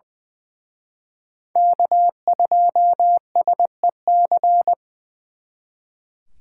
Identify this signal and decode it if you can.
Morse code, heard as a keyed tone, about 24 wpm, 700 Hz. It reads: K2SEC
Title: Speed Racing - USA Call Signs 20wpm